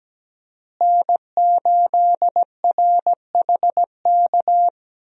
Morse code, heard as a keyed tone, about 17 wpm, 700 Hz. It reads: N8RHK
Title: USA Call Signs - 5 Characters - Repeated 3x - 17wpm